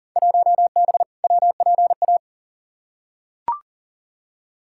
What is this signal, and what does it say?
Morse code, heard as a keyed tone, about 39 wpm, 700 Hz. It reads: 1B WPA
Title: Field Day Exchanges - 40wpm